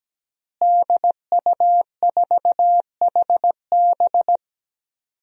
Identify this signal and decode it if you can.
Morse code, heard as a keyed tone, about 17 wpm, 700 Hz. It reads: DU4HB
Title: DXCC-100 Call Signs - 5 Characters - Repeated 3x - 17wpm